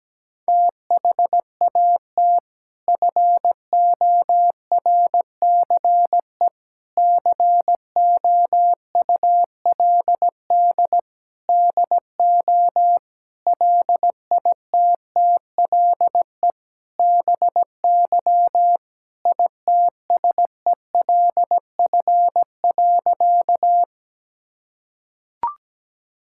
Morse code, heard as a keyed tone, about 17 wpm, 700 Hz. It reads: THAT FORCE COULD DO LITTLE BY ITSELF.
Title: Sentences from Top 1000 Words - Review Part 5 - 17wpm